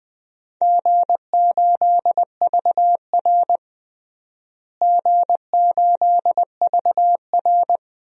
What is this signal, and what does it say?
Morse code, heard as a keyed tone, about 20 wpm, 700 Hz. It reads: G8VR G8VR
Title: DXCC-100 Call Signs - 4 Characters - Repeated 3x - 20wpm